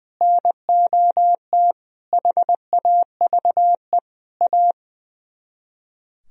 Morse code, heard as a keyed tone, about 20 wpm, 700 Hz. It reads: NOT HAVE A
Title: Speed-Racing - Sets of 4 Words 20wpm